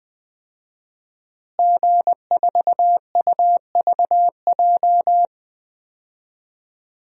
Morse code, heard as a keyed tone, about 20 wpm, 700 Hz. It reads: G4UVJ
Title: DXCC-100 Call Signs - 5 Characters - Repeated 3x - 20wpm